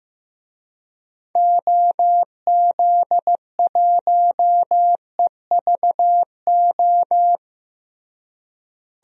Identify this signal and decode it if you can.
Morse code, heard as a keyed tone, about 15 wpm, 700 Hz. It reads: OZ1EVO
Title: DXCC-100 Call Signs - 6 Characters - Repeated 2x - 15wpm